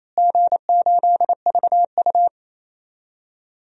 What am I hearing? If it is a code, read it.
Morse code, heard as a keyed tone, about 28 wpm, 700 Hz. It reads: G8VU
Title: DXCC-100 Call Signs - 4 Characters - Repeated 2x - 28wpm